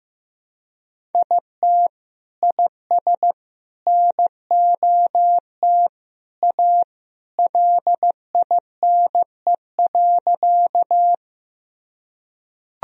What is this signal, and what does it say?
Morse code, heard as a keyed tone, about 15 wpm, 700 Hz. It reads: IT IS NOT A LINE.
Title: Sentences from Top 1000 Words - Review Part 1 - 15wpm